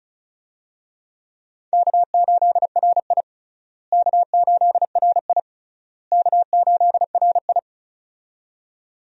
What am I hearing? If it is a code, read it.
Morse code, heard as a keyed tone, about 36 wpm, 700 Hz. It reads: K8RI K8RI K8RI
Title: USA Call Signs - 4 Characters - Repeated 3x - 35wpm